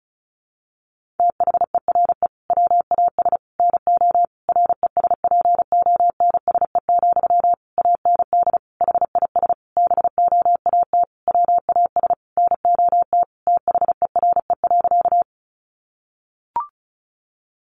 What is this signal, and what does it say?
Morse code, heard as a keyed tone, about 34 wpm, 700 Hz. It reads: THERE WAS NO RESPONSE, AND HIS BOAT WAS NOT THERE.
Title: Sentences from Top 1400 Words 35wpm